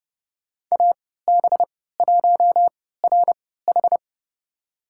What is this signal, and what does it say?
Morse code, heard as a keyed tone, about 31 wpm, spaced as for 17 wpm, 700 Hz. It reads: AB1RH
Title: USA Call Signs - 5 Characters - 3x Character Spacing - 30wpm